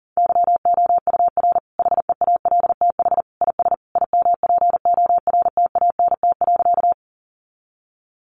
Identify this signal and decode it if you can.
Morse code, heard as a keyed tone, about 40 wpm, 700 Hz. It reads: YOUR HEALTH IS IMPORTANT.
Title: Sentences from Top 600 Words 40wpm